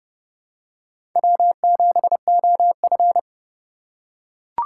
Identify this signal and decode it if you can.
Morse code, heard as a keyed tone, about 30 wpm, 700 Hz. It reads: W7OF
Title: USA Call Signs - 2x Character Spacing - 30wpm